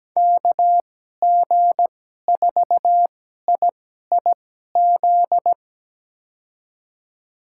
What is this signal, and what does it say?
Morse code, heard as a keyed tone, about 17 wpm, spaced as for 12 wpm, 700 Hz. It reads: KG4IIZ
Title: USA Call Signs - 6 Characters - 2x Character Spacing - 17wpm